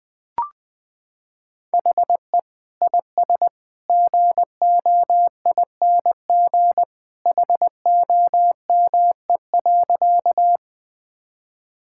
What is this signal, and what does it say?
Morse code, heard as a keyed tone, about 20 wpm, 700 Hz. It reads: HE IS GOING HOME.
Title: Sentences with 2-Word Phrases - 20wpm